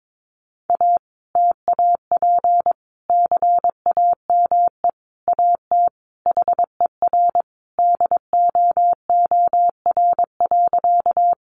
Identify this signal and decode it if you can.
Morse code, heard as a keyed tone, about 22 wpm, 700 Hz. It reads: A TAP CAME AT HER DOOR.
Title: Sentences from Top 1400 Words 22wpm